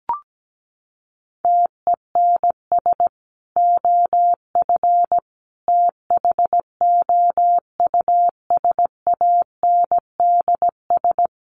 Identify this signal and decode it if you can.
Morse code, heard as a keyed tone, about 17 wpm, 700 Hz. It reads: TENS OF THOUSANDS
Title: Sets of 4 Words 17wpm